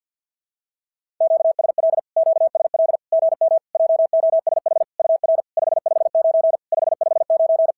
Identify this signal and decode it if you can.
Morse code, heard as a keyed tone, about 52 wpm, 650 Hz. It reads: QSL QSL GM JOSH UR 559 559
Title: SOTA QSO Practice 50wpm